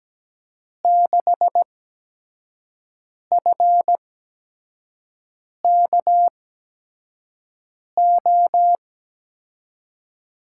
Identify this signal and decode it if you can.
Morse code, heard as a keyed tone, about 17 wpm, spaced as for 5 wpm, 700 Hz. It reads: 6FKO
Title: USA Call Signs - 6 Characters - 8x Character Spacing - 17wpm